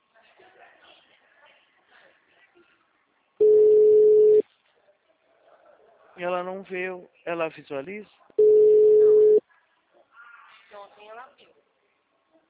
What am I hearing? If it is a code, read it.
Ringback tone (Europe, 425 Hz), with speech.
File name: out- -2006- - - .80430